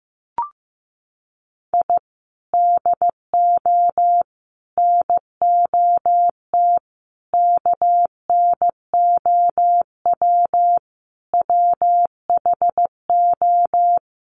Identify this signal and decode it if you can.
Morse code, heard as a keyed tone, about 15 wpm, 700 Hz. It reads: I DO NOT KNOW WHO
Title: Sets of 5 Words 15wpm